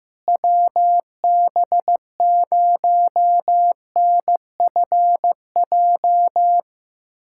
Morse code, heard as a keyed tone, about 15 wpm, 700 Hz. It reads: WB0NFJ
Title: USA Callsign DE Callsign - 6 Characters - 15wpm